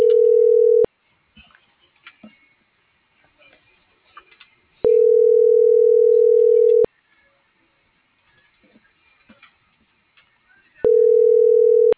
Ringback tone (North America), 440 + 480 Hz.
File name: CENTRAL MABE / out david.toribio